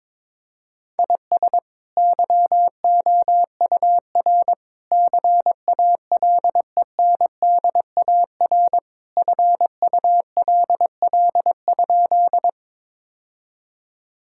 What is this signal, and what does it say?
Morse code, heard as a keyed tone, about 22 wpm, 700 Hz. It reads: IS YOUR CALENDAR FULL?
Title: Sentences from Top 1600 Words 22wpm